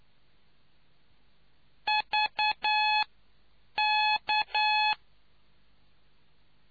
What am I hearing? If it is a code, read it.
Morse code, heard as a keyed tone, about 9 wpm, 850 Hz. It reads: V K